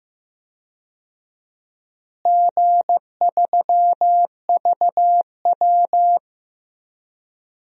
Morse code, heard as a keyed tone, about 15 wpm, 700 Hz. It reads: G3VW
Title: DXCC-100 Call Signs - 4 Characters - Repeated 3x - 15wpm